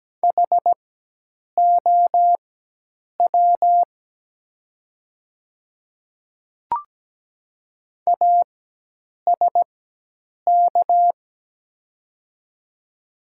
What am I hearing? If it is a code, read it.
Morse code, heard as a keyed tone, about 17 wpm, spaced as for 8 wpm, 700 Hz. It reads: HOW ASK
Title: 4x Character Spacing - Top 300 Words - 17wpm